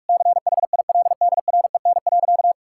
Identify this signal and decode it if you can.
Morse code, heard as a keyed tone, about 41 wpm, 700 Hz. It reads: CHILDREN.
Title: Sentences from Top 1000 Words - Review Part 6 - 45wpm